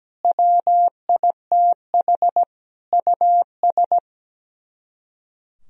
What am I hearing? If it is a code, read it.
Morse code, heard as a keyed tone, about 17 wpm, 700 Hz. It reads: WITH US
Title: Speed-Racing - Sets of 5 Words 17wpm